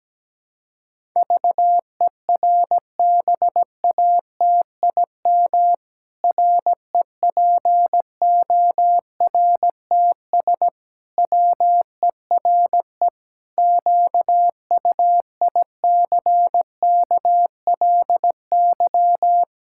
Morse code, heard as a keyed tone, about 17 wpm, 700 Hz. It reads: VERBATIM REPORTS WERE QUICKLY
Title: Pangram Challenge v2 - 17wpm